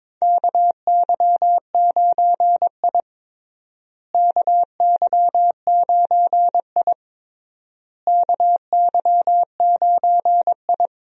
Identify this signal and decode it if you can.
Morse code, heard as a keyed tone, about 22 wpm, 700 Hz. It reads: KY9I KY9I KY9I
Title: USA Call Signs - 4 Characters - Repeated 3x - 22wpm